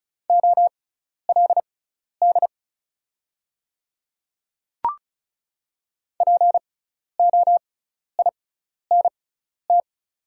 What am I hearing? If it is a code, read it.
Morse code, heard as a keyed tone, about 34 wpm, spaced as for 12 wpm, 700 Hz. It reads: OLD POINT
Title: 6x Character Spacing - Top 300 Words - 35wpm